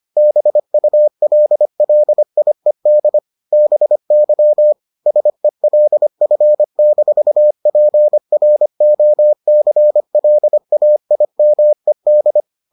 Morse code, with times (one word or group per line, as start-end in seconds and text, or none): 0.2-3.2 BULLIED
3.5-4.7 BY
5.1-12.4 SELF-PROCLAIMED